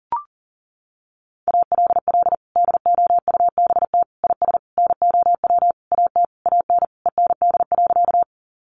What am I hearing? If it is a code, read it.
Morse code, heard as a keyed tone, about 40 wpm, 700 Hz. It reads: ALL DOUBT IS NOW AT AN END.
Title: Sentences from Top 1500 Words 40wpm